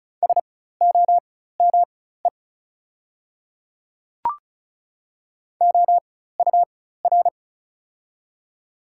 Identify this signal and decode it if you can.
Morse code, heard as a keyed tone, about 34 wpm, spaced as for 16 wpm, 690 Hz. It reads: SOME OUR
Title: 4x Character Spacing - Top 200 Words - 35wpm